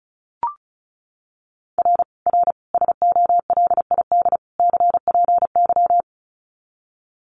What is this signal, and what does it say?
Morse code, heard as a keyed tone, about 34 wpm, 700 Hz. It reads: R R SOLID CPY
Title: Common QSO Phrases 35wpm